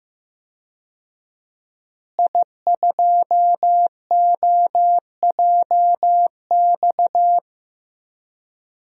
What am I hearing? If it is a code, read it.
Morse code, heard as a keyed tone, about 15 wpm, 700 Hz. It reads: I2OJX
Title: DXCC-100 Call Signs - 5 Characters - Repeated 3x - 15wpm